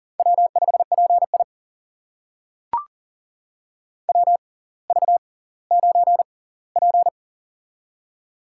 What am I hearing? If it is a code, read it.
Morse code, heard as a keyed tone, about 39 wpm, spaced as for 20 wpm, 700 Hz. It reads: W5PI WV9P
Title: USA Call Signs - 4 Characters - 6x Character Spacing - 40wpm